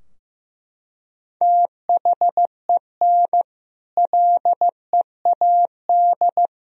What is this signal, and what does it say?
Morse code, heard as a keyed tone, about 15 wpm, 700 Hz. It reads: THEN LEAD
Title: Sentences from Top 1000 Words - Review Part 2 - 15wpm